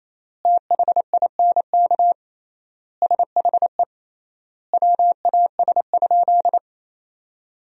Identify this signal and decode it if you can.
Morse code, heard as a keyed tone, about 27 wpm, 700 Hz. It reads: THINK SHE WAS?
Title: 3x Word-Spacing - Sentences from Top 200 Words - 28wpm